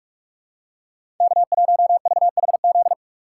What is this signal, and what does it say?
Morse code, heard as a keyed tone, about 48 wpm, 700 Hz. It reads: K1VHZ
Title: USA Call Signs - 5 Characters - 4x Character Spacing - 45wpm